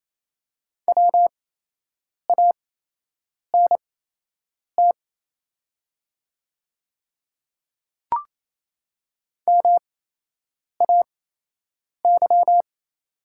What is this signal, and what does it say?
Morse code, heard as a keyed tone, about 28 wpm, spaced as for 8 wpm, 700 Hz. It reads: WANT MAY
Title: 8x Character Spacing - Top 300 Words - 28wpm